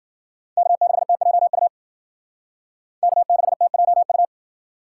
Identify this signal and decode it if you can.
Morse code, heard as a keyed tone, about 51 wpm, 700 Hz. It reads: K6TJU K6TJU